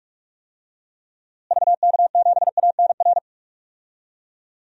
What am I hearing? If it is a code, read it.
Morse code, heard as a keyed tone, about 43 wpm, 700 Hz. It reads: VK7ANR
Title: DXCC-100 Call Signs - 6 Characters - Repeated 2x - 45wpm